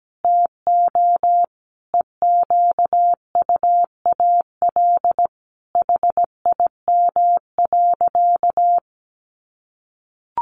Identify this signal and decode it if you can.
Morse code, heard as a keyed tone, about 17 wpm, 700 Hz. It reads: TO EQUAL HIM.
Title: Sentences from Top 1400 Words 17wpm